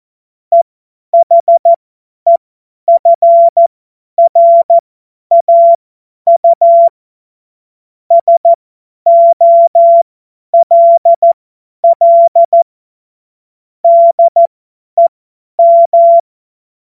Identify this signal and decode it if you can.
Morse code, heard as a keyed tone, about 14 wpm, spaced as for 10 wpm, 680 Hz. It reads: EHEFRAU SOLL DEM
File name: Morsekurs / Witze / Witz 23 Januar 2024 50 F70